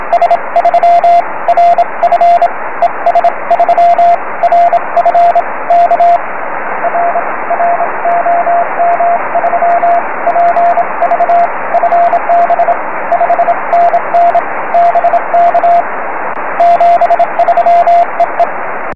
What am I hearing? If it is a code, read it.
Morse code, heard as a keyed tone, about 27 wpm, spaced as for 23 wpm, 690 Hz, with radio noise. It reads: S3RF ES3RF K R R OM3PV FB 5NN BK 73 EE
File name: om3pv 70mhz / zaznam / es3rf